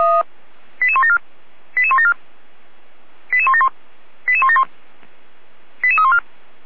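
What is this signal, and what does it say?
Five-tone selective call (ZVEI), heard as quick runs of five tones, tones of about 70 ms. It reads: 88174 88174 88171 88171 88314
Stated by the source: ZVEI five-tone selective call 88012 received via SDR, from the audio.f85360.feuerwehr-orlamuende.org fire brigade server.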